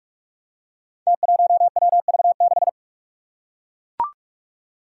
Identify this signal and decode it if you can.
Morse code, heard as a keyed tone, about 41 wpm, 700 Hz. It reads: T1WV6
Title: Five Challenging Letter-Number Combinations 45wpm